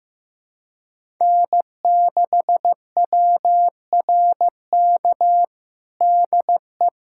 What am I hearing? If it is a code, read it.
Morse code, heard as a keyed tone, about 15 wpm, 700 Hz. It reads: N6WRK DE
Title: USA Callsign DE Callsign 15wpm